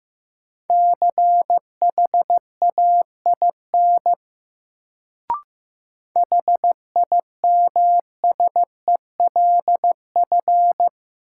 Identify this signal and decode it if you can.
Morse code, heard as a keyed tone, about 15 wpm, 700 Hz. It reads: CHAIN HIMSELF
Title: Sentences from Top 1500 Words - New Word Drill Alt - 15wpm